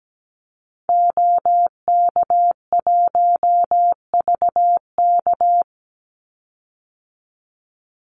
Morse code, heard as a keyed tone, about 17 wpm, 700 Hz. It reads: OK1VK
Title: Call Signs - 17wpm